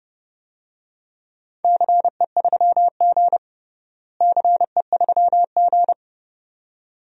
Morse code, heard as a keyed tone, about 32 wpm, 700 Hz. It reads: CE3G CE3G
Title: DXCC-100 Call Signs - 4 Characters - Repeated 2x - 30wpm